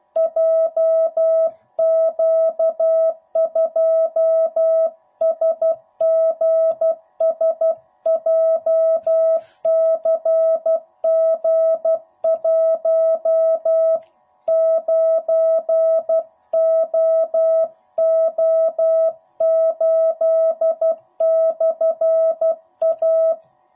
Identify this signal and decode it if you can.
Morse code, heard as a keyed tone, about 12 wpm, 660 Hz. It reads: JQ2SGS JCG19OO8/A
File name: jh8cbh / 2021cw / cw05b